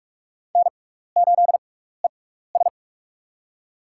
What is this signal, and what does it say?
Morse code, heard as a keyed tone, about 43 wpm, spaced as for 15 wpm, 700 Hz. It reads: N8ES
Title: USA Call Signs - 4 Characters - 6x Character Spacing - 45wpm